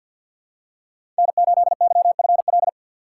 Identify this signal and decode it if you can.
Morse code, heard as a keyed tone, about 50 wpm, 700 Hz. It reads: N8YFL
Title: USA Call Signs - 5 Characters - 3x Character Spacing - 50wpm